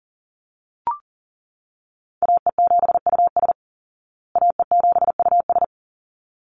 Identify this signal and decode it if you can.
Morse code, heard as a keyed tone, about 40 wpm, 700 Hz. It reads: AE7US AE7US
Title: Call Signs Repeated 3x - 40wpm